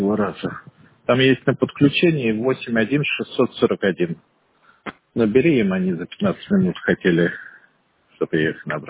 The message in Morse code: none

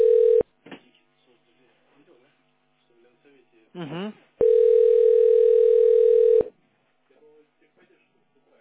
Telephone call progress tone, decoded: ringback tone

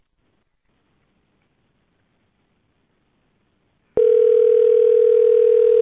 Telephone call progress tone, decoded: ringback tone

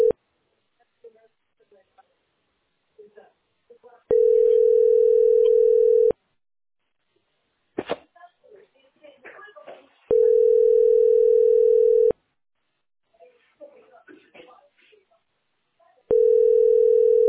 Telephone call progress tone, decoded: ringback tone

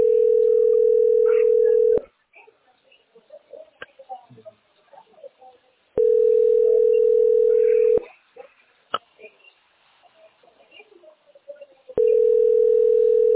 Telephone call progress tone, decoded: ringback tone

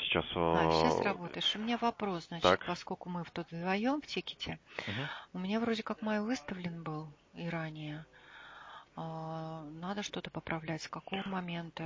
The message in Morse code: none

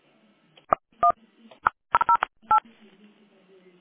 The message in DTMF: 108

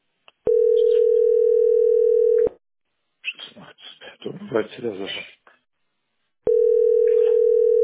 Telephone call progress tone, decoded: ringback tone

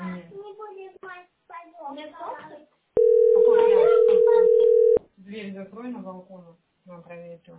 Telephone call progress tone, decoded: ringback tone

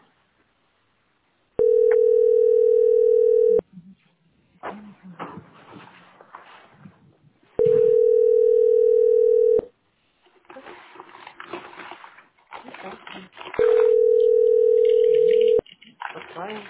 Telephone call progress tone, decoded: ringback tone

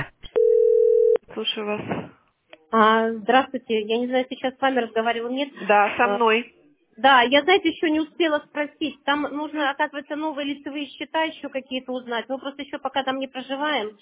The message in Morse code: none

